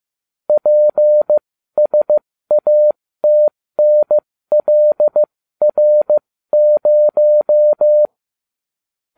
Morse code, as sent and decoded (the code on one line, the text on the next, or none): .--. ... .- - -. .-.. .-. -----
PSATNLR0